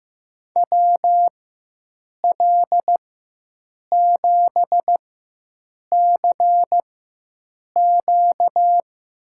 Morse code, heard as a keyed tone, about 15 wpm, spaced as for 7 wpm, 700 Hz. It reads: WL7CQ
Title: USA Call Signs - 4x Character Spacing - 15wpm